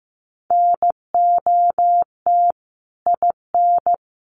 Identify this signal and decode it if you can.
Morse code, heard as a keyed tone, about 15 wpm, 700 Hz. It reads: NOT IN
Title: Sentences from Top 1400 Words 15wpm